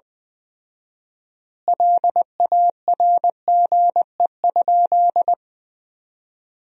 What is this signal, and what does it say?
Morse code, heard as a keyed tone, about 20 wpm, 700 Hz. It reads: LARGE?
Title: 4x Word-Spacing - Sentences from Top 200 Words - 20wpm